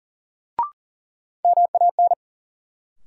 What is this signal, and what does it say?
Morse code, heard as a keyed tone, about 39 wpm, 700 Hz. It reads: MAN
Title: Sentences from Top 200 Words - New Word Drill - 40wpm